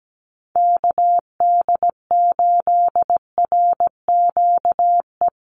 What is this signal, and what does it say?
Morse code, heard as a keyed tone, about 17 wpm, 700 Hz. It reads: KD8RQE
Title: Call Signs Repeated 2x - 17wpm